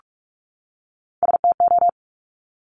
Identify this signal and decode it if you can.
Morse code, heard as a keyed tone, about 46 wpm, 700 Hz. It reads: STO
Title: Top Three Letter Combinations in Words 45wpm